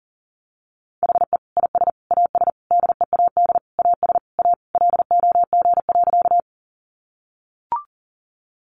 Morse code, heard as a keyed tone, about 40 wpm, 700 Hz. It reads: HE IS AS DEAD AS A LOG.
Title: Sentences from Top 1500 Words 40wpm